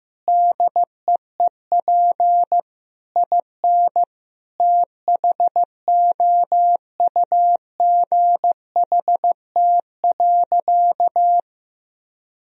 Morse code, heard as a keyed tone, about 15 wpm, 700 Hz. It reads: DEEP IN THOUGHT.
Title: Sentences from Top 1000 Words - Review Part 3 - 15wpm